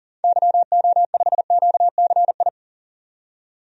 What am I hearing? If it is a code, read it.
Morse code, heard as a keyed tone, about 40 wpm, 700 Hz. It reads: YO5QCI
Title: DXCC-100 Call Signs - 6 Characters - 40wpm